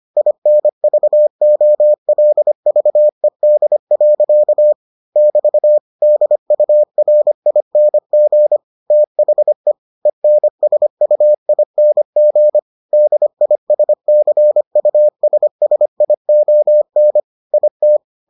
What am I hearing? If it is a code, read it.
Morse code, heard as a keyed tone, about 25 wpm, 600 Hz. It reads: INVOLVED. <BT> DURING THE ENSUING DISCUSSION IT